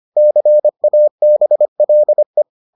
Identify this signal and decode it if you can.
Morse code, heard as a keyed tone, about 25 wpm, 600 Hz. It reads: CABLE